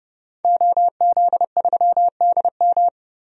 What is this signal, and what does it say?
Morse code, heard as a keyed tone, about 30 wpm, 700 Hz. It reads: OZ3DM
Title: DXCC-100 Call Signs - 5 Characters - 30wpm